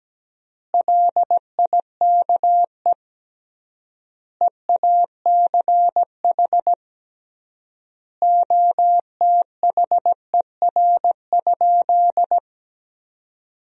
Morse code, heard as a keyed tone, about 17 wpm, 700 Hz. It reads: LIKE EACH OTHER?
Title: 3x Word-Spacing - Sentences from Top 200 Words - 17wpm